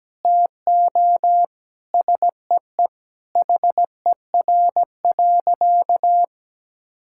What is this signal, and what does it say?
Morse code, heard as a keyed tone, about 17 wpm, 700 Hz. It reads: TO SEE HER.
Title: Sentences from Top 1000 Words - Review Part 5 - 17wpm